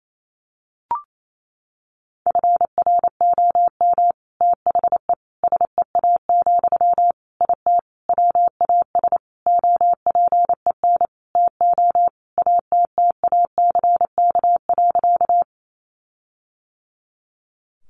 Morse code, heard as a keyed tone, about 28 wpm, 700 Hz. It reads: FROM THE SEA, IT WAS OPEN TO ATTACK.
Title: Sentences from Top 1500 Words 28wpm